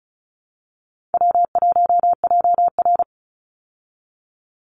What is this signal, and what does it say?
Morse code, heard as a keyed tone, about 34 wpm, 700 Hz. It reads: W1JR
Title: Easy Call Signs Repeated 2x - 35wpm